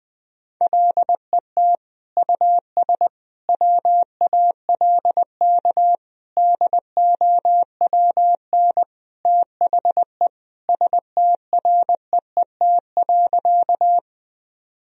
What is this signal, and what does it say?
Morse code, heard as a keyed tone, about 20 wpm, 700 Hz. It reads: LET US WALK DOWN THE STREET.
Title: Sentences from Top 1000 Words - Review Part 3 - 20wpm